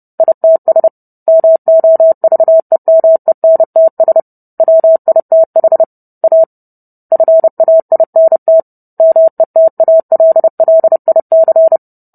Morse code, heard as a keyed tone, about 30 wpm, 650 Hz. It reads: ITS MOVEMENTS WITH A FAINT METALLIC